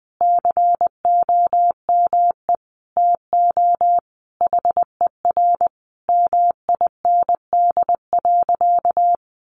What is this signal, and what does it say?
Morse code, heard as a keyed tone, about 19 wpm, 700 Hz. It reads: COME TO HER MIND.
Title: Sentences from Top 1300 Words 20wpm